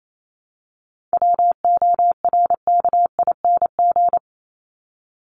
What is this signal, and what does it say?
Morse code, heard as a keyed tone, about 28 wpm, 700 Hz. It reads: WORKING
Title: ING Suffix - Top 100 Words with Base Words 28wpm